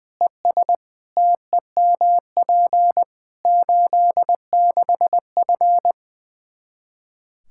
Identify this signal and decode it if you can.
Morse code, heard as a keyed tone, about 21 wpm, 700 Hz. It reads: ES TEMP 86F
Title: Practice QSO - Exchange 2 - Info - 20wpm